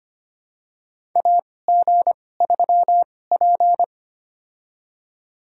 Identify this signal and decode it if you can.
Morse code, heard as a keyed tone, about 25 wpm, spaced as for 18 wpm, 700 Hz. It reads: AG3P
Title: USA Call Signs - 4 Characters - 2x Character Spacing - 25wpm